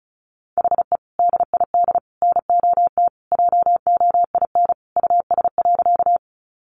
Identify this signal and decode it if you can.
Morse code, heard as a keyed tone, about 34 wpm, 700 Hz. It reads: HE DID NOT JOIN US.
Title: Sentences from Top 1400 Words 35wpm